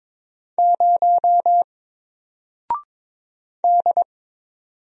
Morse code, heard as a keyed tone, about 22 wpm, 700 Hz. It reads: 0 D
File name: Single-Letter-Number-22wpm